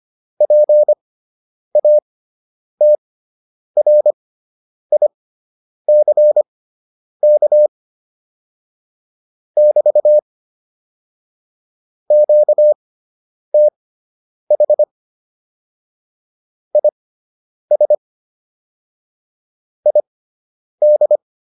Morse code, heard as a keyed tone, about 25 wpm, spaced as for 9 wpm, 600 Hz. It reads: PATRICK <BT> QTH IS ID
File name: QSO 105 09